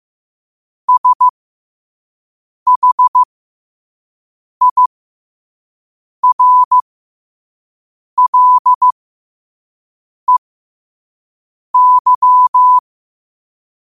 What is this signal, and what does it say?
Morse code, heard as a keyed tone, about 15 wpm, spaced as for 5 wpm, 1000 Hz. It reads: SHIRLEY
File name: qotdfiles / 5wpm